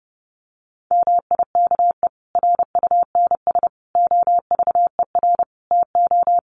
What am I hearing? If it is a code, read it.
Morse code, heard as a keyed tone, about 30 wpm, 700 Hz. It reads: MIKE RUNS OVER TO
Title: Additional Easy Phrases 30wpm